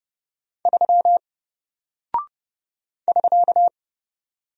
Morse code, heard as a keyed tone, about 30 wpm, 700 Hz. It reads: 3 <SK>
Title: Single Character (Letter:Number:Punctuation:Prosign) Alt 30wpm